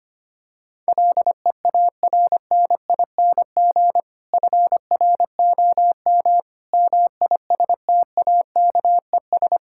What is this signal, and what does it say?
Morse code, heard as a keyed tone, about 25 wpm, 700 Hz. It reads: LEARNING FROM MISTAKES
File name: practice / 25wpm / Difficult Sentences with Hints - 25wpm